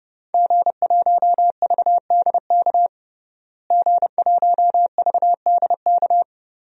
Morse code, heard as a keyed tone, about 30 wpm, 700 Hz. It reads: G1VDK G1VDK
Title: DXCC-100 Call Signs - 5 Characters - Repeated 3x - 30wpm